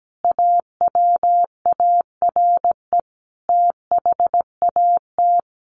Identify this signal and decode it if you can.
Morse code, heard as a keyed tone, about 17 wpm, 700 Hz. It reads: AWARE THAT
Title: Sentences from Top 1200 Words 17wpm